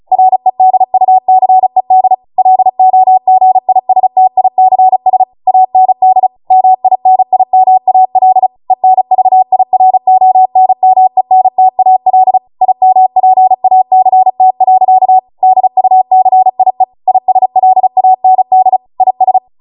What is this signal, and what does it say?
Morse code, heard as a keyed tone, about 36 wpm, 750 Hz. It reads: REDUCED LOGISTICS AND MINIMAL ENVIRONMENTAL IMPACT. DUCIE ISLAND IS